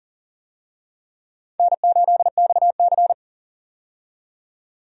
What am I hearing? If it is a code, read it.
Morse code, heard as a keyed tone, about 40 wpm, 700 Hz. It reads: N8XC
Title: USA Call Signs - 4 Characters - 40wpm